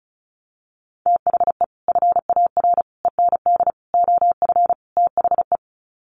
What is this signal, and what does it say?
Morse code, heard as a keyed tone, about 34 wpm, 700 Hz. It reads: THE FAR END OF THE
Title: Sets of 5 Words 35wpm